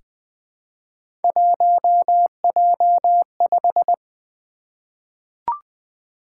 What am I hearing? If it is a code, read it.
Morse code, heard as a keyed tone, about 20 wpm, 700 Hz. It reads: 1J5
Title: Three Letter-Number Combinations - 20wpm